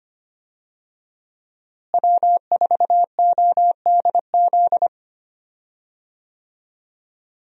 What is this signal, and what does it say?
Morse code, heard as a keyed tone, about 25 wpm, 700 Hz. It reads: W4ODZ
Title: USA Call Signs - 5 Characters - Repeated 3x - 25wpm